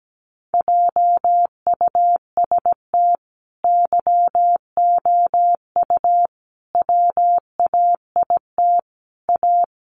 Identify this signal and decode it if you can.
Morse code, heard as a keyed tone, about 17 wpm, 700 Hz. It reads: JUST YOU WAIT A
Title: Sentences from Top 900 Words 17wpm